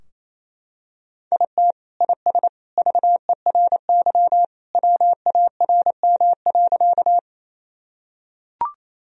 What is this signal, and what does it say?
Morse code, heard as a keyed tone, about 28 wpm, 700 Hz. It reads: IT IS VERY WARM.
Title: Sentences from Top 1000 Words - Review Part 2 - 28wpm